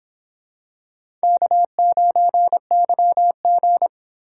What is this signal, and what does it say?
Morse code, heard as a keyed tone, about 26 wpm, 700 Hz. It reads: K9YG